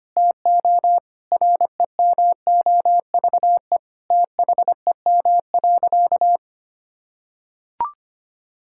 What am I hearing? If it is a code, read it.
Morse code, heard as a keyed tone, about 25 wpm, 700 Hz. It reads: TO REMOVE THEM.
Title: Sentences from Top 1000 Words - Review Part 5 - 25wpm